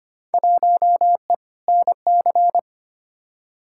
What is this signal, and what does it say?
Morse code, heard as a keyed tone, about 25 wpm, 700 Hz. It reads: 1E NC
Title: Field Day Exchanges - 25wpm